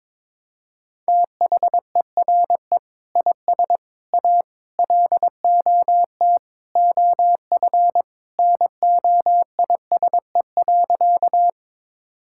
Morse code, heard as a keyed tone, about 22 wpm, 700 Hz. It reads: THERE IS A LOT OF NOISE.
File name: practice / 22wpm / Sentences with 2-Word Phrases - 22wpm